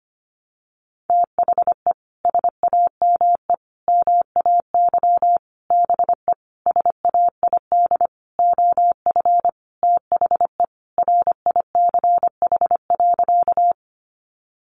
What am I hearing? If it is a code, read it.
Morse code, heard as a keyed tone, about 25 wpm, 700 Hz. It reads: THE SAME MAY BE SAID OF THE RICH.
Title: Sentences from Top 1100 Words 25wpm